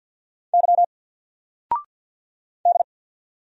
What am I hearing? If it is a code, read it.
Morse code, heard as a keyed tone, about 49 wpm, 700 Hz. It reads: Y D
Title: Single Character (Letter:Number:Punctuation:Prosign) 50wpm